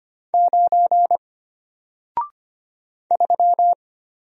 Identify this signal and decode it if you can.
Morse code, heard as a keyed tone, about 25 wpm, 700 Hz. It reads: 9 3
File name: Single-Letter-Number-25wpm